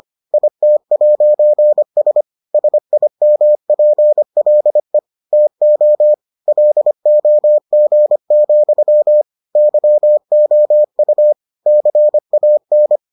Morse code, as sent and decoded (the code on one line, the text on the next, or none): .. - .----. ... / ... .. -- .--. .-.. / - --- / .-.. --- --. --..-- / -.-- --- ..- / -.-. .- -.
IT'S SIMPLE TO LOG, YOU CAN